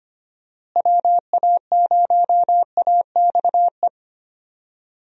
Morse code, heard as a keyed tone, about 25 wpm, 700 Hz. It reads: WA0AXE
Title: USA Call Signs - 6 Characters - Repeated 2x - 25wpm